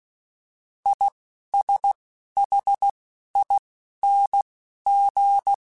Morse code, heard as a keyed tone, about 16 wpm, spaced as for 12 wpm, 800 Hz. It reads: ISHING